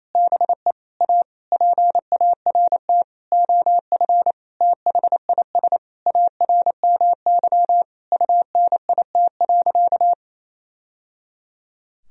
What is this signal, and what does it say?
Morse code, heard as a keyed tone, about 28 wpm, 700 Hz. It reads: BE A PART OF THIS ARMY UNIT.
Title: Sentences from Top 1000 Words - Review Part 2 - 28wpm